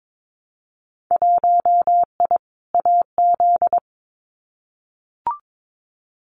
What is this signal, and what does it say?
Morse code, heard as a keyed tone, about 22 wpm, 700 Hz. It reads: 1I AZ
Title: Winter Field Day Exchanges 22wpm